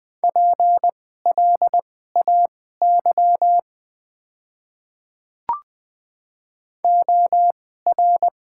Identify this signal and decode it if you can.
Morse code, heard as a keyed tone, about 20 wpm, spaced as for 15 wpm, 700 Hz. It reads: PLAY OR
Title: 2x Character Spacing - Top 300 Words - 20wpm